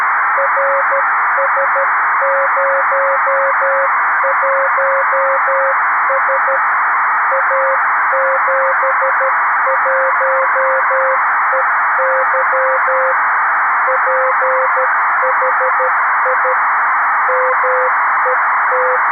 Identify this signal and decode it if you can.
Morse code, heard as a keyed tone, about 13 wpm, 500 Hz, with noise. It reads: RS01S A71EY PHI MET
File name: arissat / recordings / ARISSat 11Sept2011 1354utc 60frames